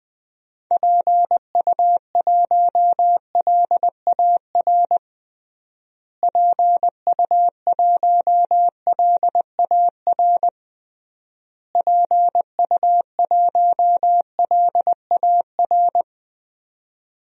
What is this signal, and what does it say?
Morse code, heard as a keyed tone, about 20 wpm, 700 Hz. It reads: PU1LAR PU1LAR PU1LAR
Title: DXCC-100 Call Signs - 6 Characters - Repeated 3x - 20wpm